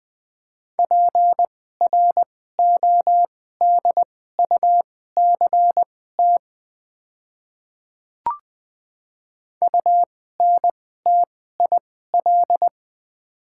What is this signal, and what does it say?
Morse code, heard as a keyed tone, about 20 wpm, spaced as for 14 wpm, 700 Hz. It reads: PRODUCT UNTIL
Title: 2x Character Spacing - Top 500 Words - 20wpm